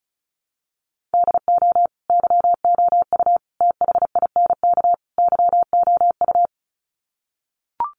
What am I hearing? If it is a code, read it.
Morse code, heard as a keyed tone, about 34 wpm, 700 Hz. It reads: DO YOU THINK YOU
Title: Sets of 4 Words 35wpm